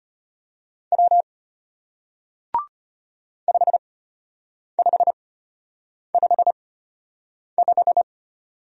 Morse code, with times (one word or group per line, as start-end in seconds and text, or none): 0.9-1.2 W
3.5-8.0 5555